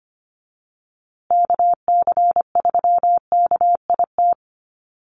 Morse code, heard as a keyed tone, about 25 wpm, 700 Hz. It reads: KC3KIT
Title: USA Call Signs Repeated 3x 25wpm